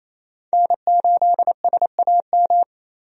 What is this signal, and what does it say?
Morse code, heard as a keyed tone, about 28 wpm, 700 Hz. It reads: N8SAM
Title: USA Call Signs - 5 Characters - 28wpm